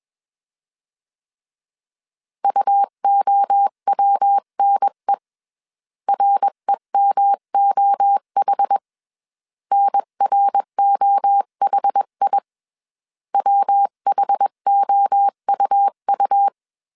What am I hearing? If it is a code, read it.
Morse code, heard as a keyed tone, about 21 wpm, 800 Hz. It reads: UOWNE REMOH NROHI WHOUU
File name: 20wpm